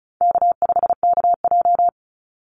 Morse code, heard as a keyed tone, about 34 wpm, 700 Hz. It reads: K5KJ
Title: Call Signs - 35wpm